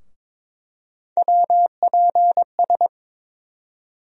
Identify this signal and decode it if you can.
Morse code, heard as a keyed tone, about 22 wpm, 700 Hz. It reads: WPS
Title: USA Call Sign Postfixes - 3 Letters 22wpm